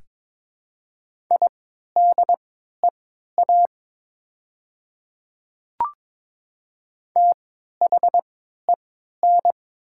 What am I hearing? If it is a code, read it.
Morse code, heard as a keyed tone, about 22 wpm, spaced as for 13 wpm, 700 Hz. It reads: IDEA THEN
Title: 3x Character Spacing - Top 300 Words - 22wpm